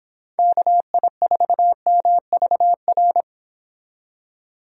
Morse code, heard as a keyed tone, about 25 wpm, 700 Hz. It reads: KI4MVR